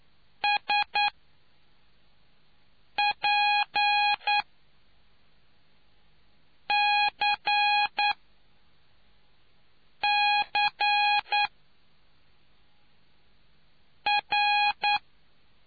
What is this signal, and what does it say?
Morse code, heard as a keyed tone, about 9 wpm, spaced as for 3 wpm, 830 Hz. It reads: SPCCR